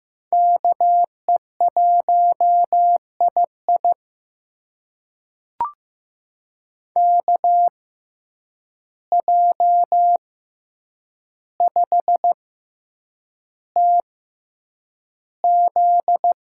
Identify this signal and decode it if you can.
Morse code, heard as a keyed tone, about 15 wpm, spaced as for 8 wpm, 700 Hz. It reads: KE1II KJ5TZ
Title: USA Call Signs - 5 Characters - 6x Character Spacing - 15wpm